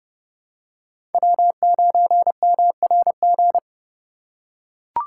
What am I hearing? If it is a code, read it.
Morse code, heard as a keyed tone, about 27 wpm, 700 Hz. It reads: W9MRG